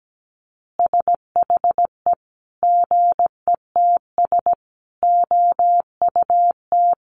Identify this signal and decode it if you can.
Morse code, heard as a keyed tone, about 17 wpm, 700 Hz. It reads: SHE GETS OUT
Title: Easy Phrases 17wpm